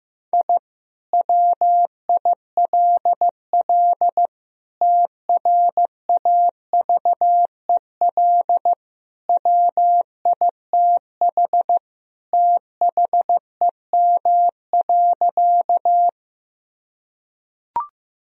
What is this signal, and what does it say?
Morse code, heard as a keyed tone, about 15 wpm, 700 Hz. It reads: I WILL TRAVEL WITH THEM.
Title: Sentences from Top 1000 Words - Review Part 2 - 15wpm